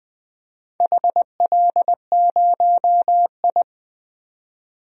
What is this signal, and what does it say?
Morse code, heard as a keyed tone, about 20 wpm, 700 Hz. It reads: HL0I
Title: DXCC-100 Call Signs - 4 Characters - 20wpm